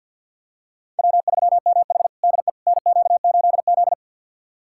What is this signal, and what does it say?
Morse code, heard as a keyed tone, about 48 wpm, 690 Hz. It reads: W3MH DE NQ8Z
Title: USA Callsign DE Callsign - 4 Characters - 50wpm